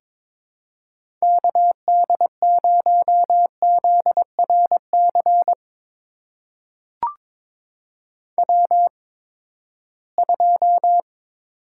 Morse code, heard as a keyed tone, about 22 wpm, spaced as for 15 wpm, 700 Hz. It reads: KD0ZRC W2
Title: USA Call Signs - 8x Character Spacing - 22wpm